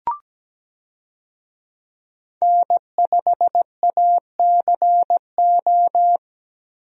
Morse code, heard as a keyed tone, about 17 wpm, 700 Hz. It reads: N5ACO
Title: USA Call Signs - 5 Characters - Repeated 3x - 17wpm